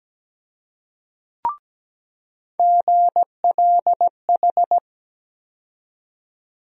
Morse code, heard as a keyed tone, about 17 wpm, 700 Hz. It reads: GLH
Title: USA Call Sign Postfixes - 3 Letters 17wpm